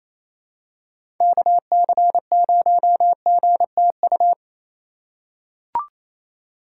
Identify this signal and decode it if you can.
Morse code, heard as a keyed tone, about 27 wpm, 700 Hz. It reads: KC0GTU
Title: USA Call Signs - 6 Characters - 4x Character Spacing - 28wpm